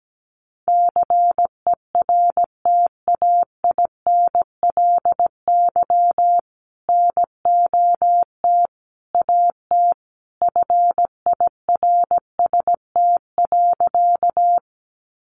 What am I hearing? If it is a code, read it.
Morse code, heard as a keyed tone, about 17 wpm, 700 Hz. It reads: CERTAINLY NOT AT FIRST.
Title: Sentences from Top 900 Words 17wpm